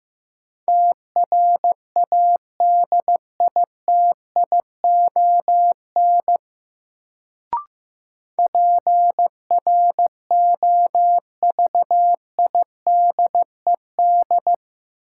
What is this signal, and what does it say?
Morse code, heard as a keyed tone, about 15 wpm, 700 Hz. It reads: TRADITION PROVIDED
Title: Sentences from Top 1500 Words - New Word Drill - 15wpm